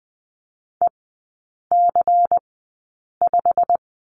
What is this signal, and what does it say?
Morse code, heard as a keyed tone, about 20 wpm, 700 Hz. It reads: E C 5
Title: 114 Lesson 19e - Letter w - Three Characters 20wpm